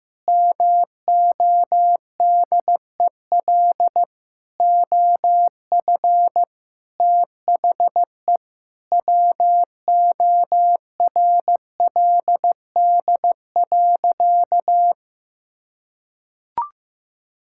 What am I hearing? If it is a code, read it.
Morse code, heard as a keyed tone, about 14 wpm, 700 Hz. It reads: MODEL OF THE WORLD.
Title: Sentences from Top 1000 Words - Review Part 5 - 15wpm